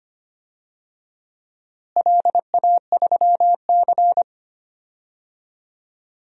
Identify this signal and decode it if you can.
Morse code, heard as a keyed tone, about 25 wpm, 700 Hz. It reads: LA3C
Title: DXCC-100 Call Signs - 4 Characters - Repeated 2x - 25wpm